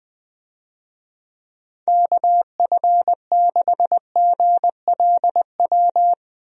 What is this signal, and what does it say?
Morse code, heard as a keyed tone, about 20 wpm, 700 Hz. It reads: KF6GLW
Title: USA Call Signs - 6 Characters - 20wpm